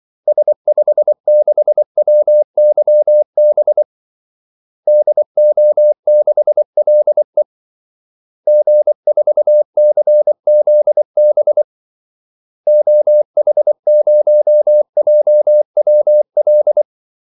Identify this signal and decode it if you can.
Morse code, heard as a keyed tone, about 24 wpm, 600 Hz. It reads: S56WYB DO6LE G4CZB OH0JWL
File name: Callsigns-24wpm